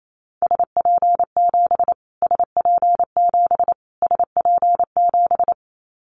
Morse code, heard as a keyed tone, about 28 wpm, 700 Hz. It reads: SP7 SP7 SP7
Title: Callsign Prefixes for Top 25 DXCC - Repeated 3x - 28wpm